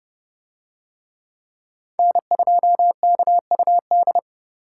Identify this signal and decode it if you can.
Morse code, heard as a keyed tone, about 27 wpm, 700 Hz. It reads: N2KUD